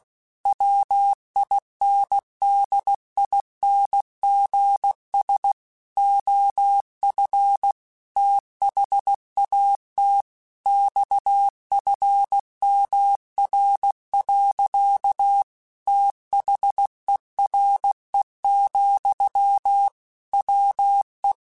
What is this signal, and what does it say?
Morse code, heard as a keyed tone, about 16 wpm, 800 Hz. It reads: WINDINGS OF THAT XFMR. THERE, WE
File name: Technical Subjects / Technical Subjects-mp3s / Sw Pwr Sup @15